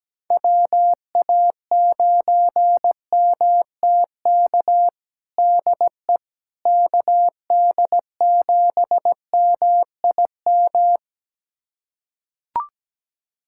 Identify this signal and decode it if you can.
Morse code, heard as a keyed tone, about 17 wpm, 700 Hz. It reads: WA9MTK DE KD7MIM
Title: USA Callsign DE Callsign - 6 Characters - 17wpm